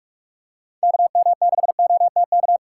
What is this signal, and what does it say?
Morse code, heard as a keyed tone, about 46 wpm, 690 Hz. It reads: KM6OTK
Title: USA Call Signs - 6 Characters - 2x Character Spacing - 45wpm